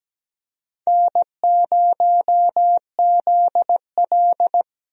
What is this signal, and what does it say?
Morse code, heard as a keyed tone, about 17 wpm, 700 Hz. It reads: N0ZL
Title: USA Call Signs - 4 Characters - Repeated 2x - 17wpm